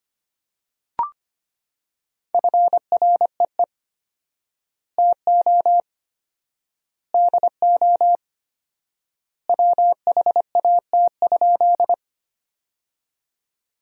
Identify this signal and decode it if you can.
Morse code, heard as a keyed tone, about 25 wpm, 700 Hz. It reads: FREE TO DO WHAT?
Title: 4x Word-Spacing - Sentences from Top 500 Words - 25wpm